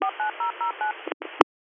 DTMF, heard as telephone touch tones, tones of about 120 ms. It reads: 49009